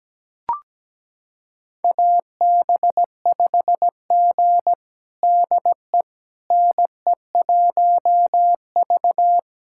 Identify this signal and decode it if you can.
Morse code, heard as a keyed tone, about 17 wpm, 700 Hz. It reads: AB5G DE NE1V
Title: USA Callsign DE Callsign - 4 Characters - 17wpm